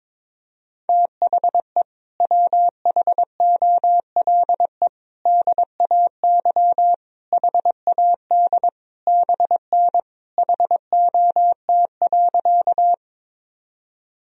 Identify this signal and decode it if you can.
Morse code, heard as a keyed tone, about 22 wpm, 700 Hz. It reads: THE WHOLE DAY HAD BN HOT.
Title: Sentences from Top 500 Words and Abbreviations 22wpm